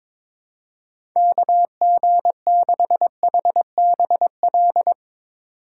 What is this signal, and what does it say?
Morse code, heard as a keyed tone, about 22 wpm, 700 Hz. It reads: KG6HBL
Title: USA Call Signs - 2x Character Spacing - 22wpm